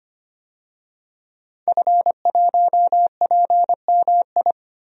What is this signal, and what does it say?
Morse code, heard as a keyed tone, about 25 wpm, 700 Hz. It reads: F1PMI
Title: DXCC-100 Call Signs - 5 Characters - 25wpm